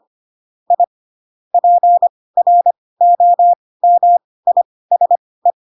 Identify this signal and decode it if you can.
Morse code, heard as a keyed tone, about 25 wpm, spaced as for 18 wpm, 700 Hz. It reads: I PROMISE